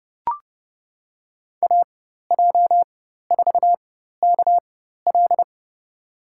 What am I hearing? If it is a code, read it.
Morse code, heard as a keyed tone, about 31 wpm, spaced as for 14 wpm, 700 Hz. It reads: AJ4KL
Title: USA Call Signs - 5 Characters - 4x Character Spacing - 30wpm